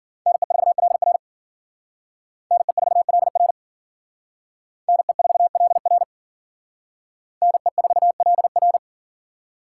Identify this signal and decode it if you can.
Morse code, heard as a keyed tone, about 53 wpm, 700 Hz. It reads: NE4LR NE4LR NE4LR NE4LR